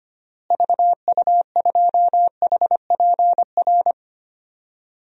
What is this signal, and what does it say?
Morse code, heard as a keyed tone, about 25 wpm, 700 Hz. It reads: VU2HPR
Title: DXCC-100 Call Signs - 6 Characters - Repeated 2x - 25wpm